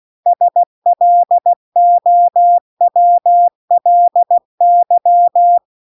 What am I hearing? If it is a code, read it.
Morse code, heard as a keyed tone, about 16 wpm, 700 Hz. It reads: SLOWLY
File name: CW-The-Big-Twoheaded-River-2-16WPM